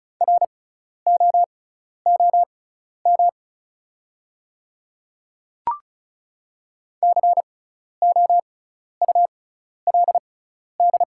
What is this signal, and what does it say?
Morse code, heard as a keyed tone, about 36 wpm, spaced as for 12 wpm, 690 Hz. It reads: ROOM COULD
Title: 6x Character Spacing - Top 300 Words - 35wpm